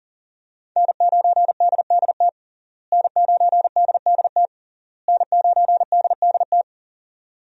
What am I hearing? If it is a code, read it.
Morse code, heard as a keyed tone, about 40 wpm, 700 Hz. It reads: N9DDT N9DDT N9DDT
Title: USA Call Signs - 5 Characters - Repeated 3x - 40wpm